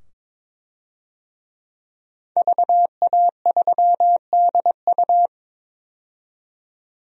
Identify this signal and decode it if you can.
Morse code, heard as a keyed tone, about 21 wpm, 700 Hz. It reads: VA3DU
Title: DXCC-100 Call Signs - 5 Characters - 22wpm